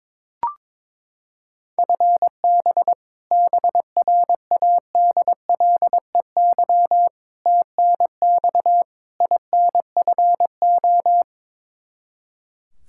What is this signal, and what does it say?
Morse code, heard as a keyed tone, about 22 wpm, 700 Hz. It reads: FB BRADLEY TNX INFO
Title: Practice QSO - Exchange 3 - More Info - 22wpm